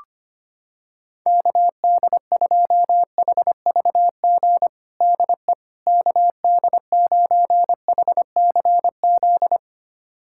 Morse code, heard as a keyed tone, about 25 wpm, 700 Hz. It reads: KD2HVG DE KD9HCZ
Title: USA Callsign DE Callsign - 6 Characters - 25wpm